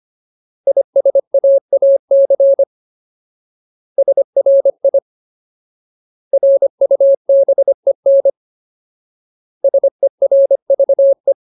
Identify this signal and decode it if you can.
Morse code, heard as a keyed tone, about 25 wpm, 550 Hz. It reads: ISAAC SRI RUBEN SERVE
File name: Ultimate Word List / 25WPM Ultimate